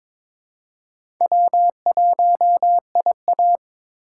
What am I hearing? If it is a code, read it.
Morse code, heard as a keyed tone, about 22 wpm, 700 Hz. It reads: W1IA
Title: USA Call Signs - 6x Character Spacing - 22wpm